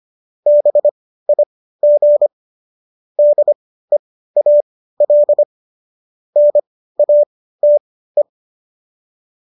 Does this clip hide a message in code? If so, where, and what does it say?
Yes: Morse code from 0.5 to 8.2 s, heard as a keyed tone, about 25 wpm, spaced as for 15 wpm, 590 Hz. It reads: BIG DEAL NATE